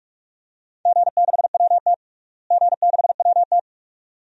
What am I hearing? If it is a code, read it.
Morse code, heard as a keyed tone, about 46 wpm, 700 Hz. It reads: G6WT G6WT
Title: DXCC-100 Call Signs - 4 Characters - Repeated 2x - 45wpm